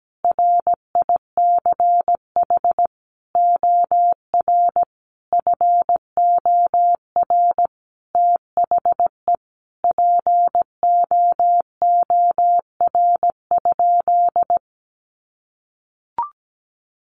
Morse code, heard as a keyed tone, about 17 wpm, 700 Hz. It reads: RICH OR FOR THE POOR?
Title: Sentences from Top 1100 Words 17wpm